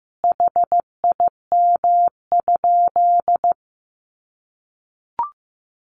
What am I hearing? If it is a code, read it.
Morse code, heard as a keyed tone, about 15 wpm, 700 Hz. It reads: HIM?
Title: Sentences from Top 100 Words 15wpm